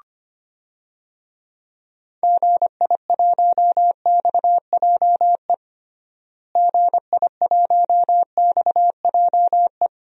Morse code, heard as a keyed tone, about 25 wpm, 700 Hz. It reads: GI1XJE GI1XJE
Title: DXCC-100 Call Signs - 6 Characters - Repeated 2x - 25wpm